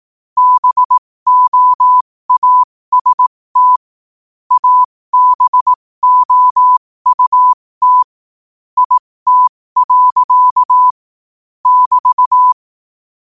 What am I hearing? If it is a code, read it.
Morse code, heard as a keyed tone, about 18 wpm, spaced as for 15 wpm, 1000 Hz. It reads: BOAST ABOUT IT. <BT>
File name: qotdfiles / 15wpm